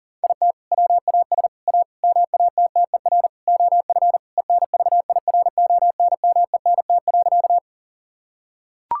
Morse code, heard as a keyed tone, about 40 wpm, 700 Hz. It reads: IT WAS A MATTER OF ENVIRONMENT.
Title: Sentences from Top 1000 Words - Review Part 5 - 40wpm